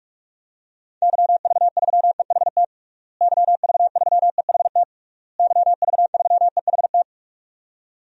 Morse code, heard as a keyed tone, about 41 wpm, 700 Hz. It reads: YV3EHT YV3EHT YV3EHT
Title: DXCC-100 Call Signs - 6 Characters - Repeated 3x - 45wpm